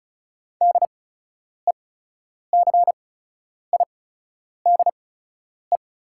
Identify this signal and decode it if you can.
Morse code, heard as a keyed tone, about 36 wpm, spaced as for 10 wpm, 690 Hz. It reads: DECIDE
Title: 8x Character Spacing - Top 500 Words - 35wpm